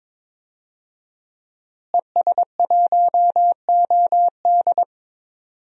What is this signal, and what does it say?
Morse code, heard as a keyed tone, about 22 wpm, 700 Hz. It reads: ES1OD
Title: DXCC-100 Call Signs - 5 Characters - Repeated 2x - 22wpm